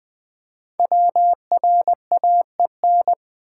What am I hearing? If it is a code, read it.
Morse code, heard as a keyed tone, about 20 wpm, 700 Hz. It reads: WRAEN
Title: Code Groups - 5 Letters - 20wpm